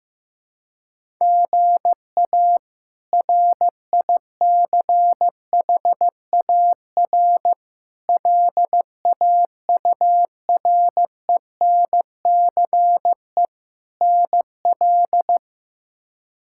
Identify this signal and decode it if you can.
Morse code, heard as a keyed tone, about 16 wpm, 700 Hz. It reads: GA RICHAR LAURENCE NL
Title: K1USN SST QSO Practice - 15wpm